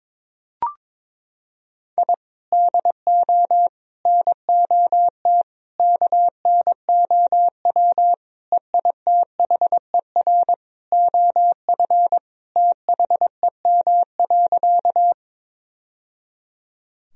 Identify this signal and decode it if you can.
Morse code, heard as a keyed tone, about 22 wpm, 700 Hz. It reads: I DO NOT KNOW EITHER OF THEM.
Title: Sentences from Top 1000 Words - Review Part 4 - 22wpm